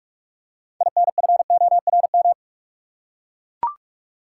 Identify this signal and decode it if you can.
Morse code, heard as a keyed tone, about 44 wpm, 700 Hz. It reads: INFORM
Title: Six Letter Words - 45wpm